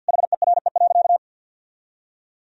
Morse code, heard as a keyed tone, about 50 wpm, 700 Hz. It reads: HERE.
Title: Sentences from Top 1000 Words - Review Part 4 - 50wpm